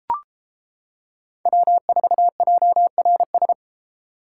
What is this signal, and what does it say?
Morse code, heard as a keyed tone, about 33 wpm, 700 Hz. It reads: W4JRS